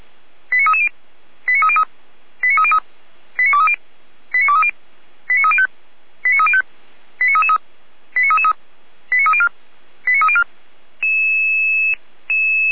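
Five-tone selective call (ZVEI), heard as quick runs of five tones, tones of about 70 ms. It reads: 89330 89303 89303 89230 89230 89305 89305 89333 89333 89304 89304
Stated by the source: ZVEI five-tone selective call 89319, received by SDR, from the feuerwehr-orlamuende.org audio server.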